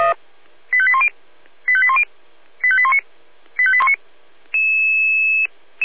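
Five-tone selective call (ZVEI), heard as quick runs of five tones, tones of about 70 ms. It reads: 85810 85810 85819 85819
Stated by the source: ZVEI five-tone selective call received with an SDR.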